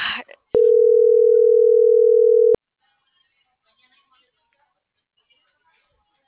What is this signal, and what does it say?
Ringback tone (North America), 440 + 480 Hz.